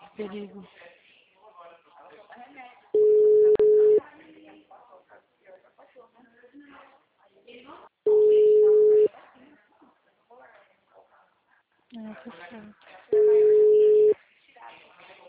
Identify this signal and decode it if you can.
Ringback tone (Europe, 425 Hz), with speech.
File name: out- -2012- - - .79388